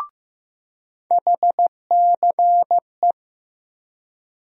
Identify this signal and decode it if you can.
Morse code, heard as a keyed tone, about 15 wpm, 700 Hz. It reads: HCE
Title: USA Call Sign Postfixes - 3 Letters 15wpm